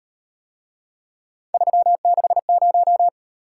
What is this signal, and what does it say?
Morse code, heard as a keyed tone, about 37 wpm, 700 Hz. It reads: 360